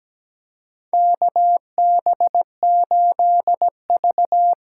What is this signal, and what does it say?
Morse code, heard as a keyed tone, about 17 wpm, 700 Hz. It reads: KB8V
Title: USA Call Signs - 6 Characters - 6x Character Spacing - 17wpm